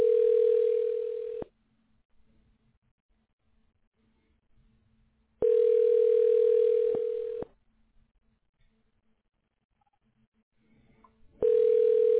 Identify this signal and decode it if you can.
Ringback tone (North America), 440 + 480 Hz.